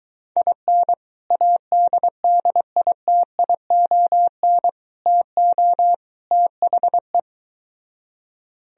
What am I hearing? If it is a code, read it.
Morse code, heard as a keyed tone, about 23 wpm, 700 Hz. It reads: IN ADDITION TO THE